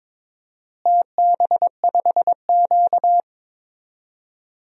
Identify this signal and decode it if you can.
Morse code, heard as a keyed tone, about 21 wpm, 700 Hz. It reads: TB5Q
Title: DXCC-100 Call Signs - 4 Characters - Repeated 3x - 22wpm